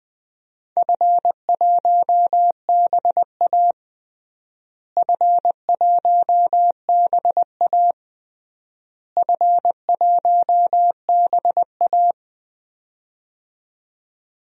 Morse code, heard as a keyed tone, about 21 wpm, 700 Hz. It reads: F1BA F1BA F1BA
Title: DXCC-100 Call Signs - 4 Characters - Repeated 3x - 20wpm